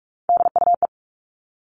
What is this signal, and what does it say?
Morse code, heard as a keyed tone, about 46 wpm, 700 Hz. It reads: DUE
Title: Three Challenging Letter-Number Combinations 45wpm